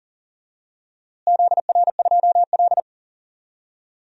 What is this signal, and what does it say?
Morse code, heard as a keyed tone, about 38 wpm, 700 Hz. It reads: ZR2L
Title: DXCC-100 Call Signs - 4 Characters - Repeated 3x - 40wpm